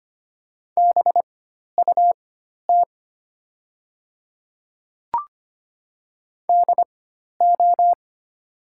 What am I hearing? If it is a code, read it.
Morse code, heard as a keyed tone, about 25 wpm, spaced as for 12 wpm, 700 Hz. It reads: BUT DO
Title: 4x Character Spacing - Top 200 Words - 25wpm